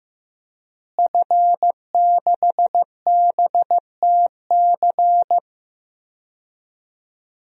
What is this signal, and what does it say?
Morse code, heard as a keyed tone, about 15 wpm, 700 Hz. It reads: F6BTC
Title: DXCC-100 Call Signs - 5 Characters - Repeated 2x - 15wpm